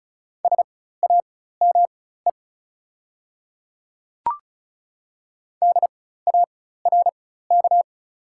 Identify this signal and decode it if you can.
Morse code, heard as a keyed tone, about 34 wpm, spaced as for 16 wpm, 690 Hz. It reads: SAME DARK
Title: 4x Character Spacing - Top 500 Words - 35wpm